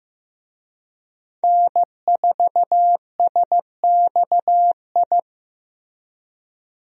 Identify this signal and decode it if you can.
Morse code, heard as a keyed tone, about 15 wpm, 700 Hz. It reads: N4SXI
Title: USA Call Signs - 5 Characters - Repeated 3x - 15wpm